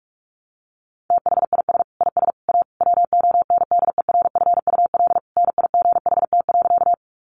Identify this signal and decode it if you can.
Morse code, heard as a keyed tone, about 44 wpm, 700 Hz. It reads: THIS IS A WONDERFUL NIGHT.
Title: Sentences from Top 1100 Words 45wpm